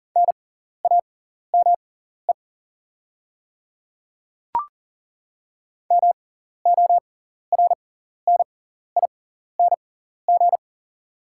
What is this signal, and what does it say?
Morse code, heard as a keyed tone, about 38 wpm, spaced as for 14 wpm, 690 Hz. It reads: NAME MORNING
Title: 6x Character Spacing - Top 500 Words - 40wpm